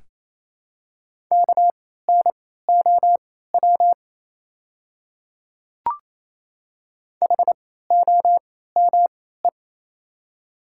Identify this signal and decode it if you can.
Morse code, heard as a keyed tone, about 28 wpm, spaced as for 16 wpm, 700 Hz. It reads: KNOW HOME
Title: 3x Character Spacing - Top 200 Words - 28wpm